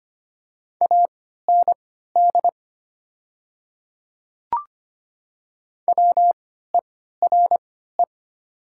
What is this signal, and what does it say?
Morse code, heard as a keyed tone, about 25 wpm, spaced as for 14 wpm, 700 Hz. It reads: AND WERE
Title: 3x Character Spacing - Top 100 Words - 25wpm